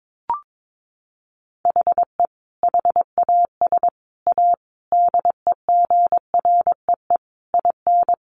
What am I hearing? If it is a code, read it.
Morse code, heard as a keyed tone, about 22 wpm, 700 Hz. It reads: HE HAS A DEGREE IN
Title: Sentences from Top 1500 Words 22wpm